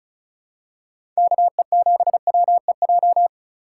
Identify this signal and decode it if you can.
Morse code, heard as a keyed tone, about 34 wpm, 700 Hz. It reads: KE7WEJ
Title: USA Call Signs - 6 Characters - 6x Character Spacing - 35wpm